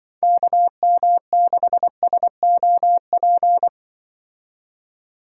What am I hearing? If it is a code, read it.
Morse code, heard as a keyed tone, about 25 wpm, 700 Hz. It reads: KM6SOP
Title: Speed Racing - USA 2x3 Call Signs 20wpm